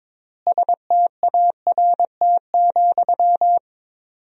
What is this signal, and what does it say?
Morse code, heard as a keyed tone, about 22 wpm, 700 Hz. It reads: START,
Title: 4x Word-Spacing - Sentences from Top 300 Words - 22wpm